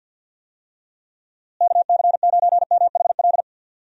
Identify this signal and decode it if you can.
Morse code, heard as a keyed tone, about 51 wpm, 700 Hz. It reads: KC9MHL
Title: USA Call Signs - 6 Characters - Repeated 2x - 50wpm